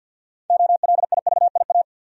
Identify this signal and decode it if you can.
Morse code, heard as a keyed tone, about 48 wpm, 700 Hz. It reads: OLIVIA
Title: USA Common Names - 50wpm